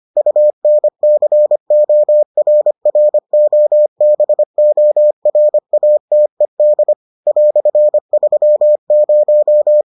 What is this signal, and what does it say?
Morse code, heard as a keyed tone, about 25 wpm, 600 Hz. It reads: UNCORROBORATED "30